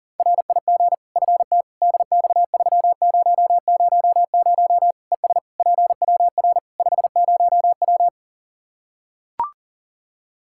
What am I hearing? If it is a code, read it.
Morse code, heard as a keyed tone, about 40 wpm, 700 Hz. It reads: RIG FT DX3000 ES PWR 50W
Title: Practice QSO - Exchange 2 - Info - 40wpm